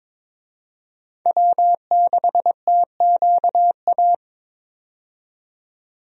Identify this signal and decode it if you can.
Morse code, heard as a keyed tone, about 22 wpm, 700 Hz. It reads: W6TQA
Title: USA Call Signs - 5 Characters - 22wpm